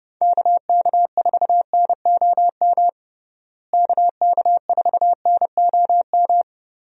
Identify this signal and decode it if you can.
Morse code, heard as a keyed tone, about 30 wpm, 700 Hz. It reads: KK4NOM KK4NOM
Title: USA Call Signs - 6 Characters - Repeated 3x - 30wpm